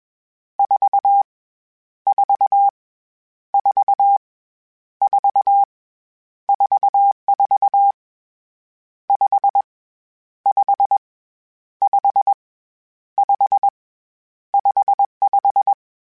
Morse code, heard as a keyed tone, about 21 wpm, 800 Hz. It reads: 4 4 4 4 44 5 5 5 5 55